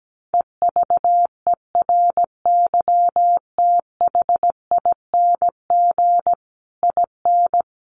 Morse code, heard as a keyed tone, about 17 wpm, 700 Hz. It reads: EVERYTHING IN
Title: A Sound Of Thunder 17wpm